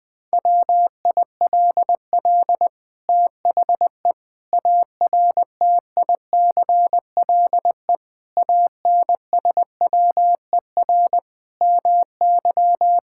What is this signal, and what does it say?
Morse code, heard as a keyed tone, about 20 wpm, 700 Hz. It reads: WILL THE ARTICLE ANSWER MY
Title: Sentences from Top 1000 Words - Review Part 5 - 20wpm